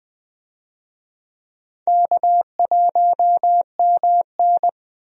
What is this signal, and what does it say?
Morse code, heard as a keyed tone, about 21 wpm, 700 Hz. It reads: K1MN
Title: USA Call Signs - 4 Characters - Repeated 3x - 20wpm